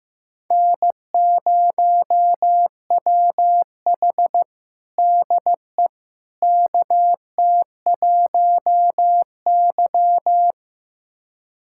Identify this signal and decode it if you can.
Morse code, heard as a keyed tone, about 15 wpm, 700 Hz. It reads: N0WH DE KT1Y
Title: USA Callsign DE Callsign - 4 Characters - 15wpm